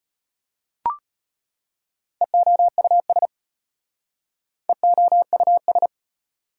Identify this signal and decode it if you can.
Morse code, heard as a keyed tone, about 33 wpm, 690 Hz. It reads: EOUS EOUS